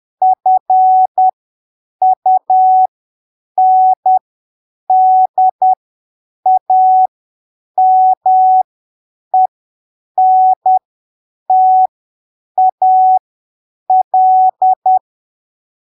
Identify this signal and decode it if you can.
Morse code, heard as a keyed tone, about 10 wpm, spaced as for 7 wpm, 750 Hz. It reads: FUNDAMENTAL
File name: arquivos / texto / texto 09 10ppm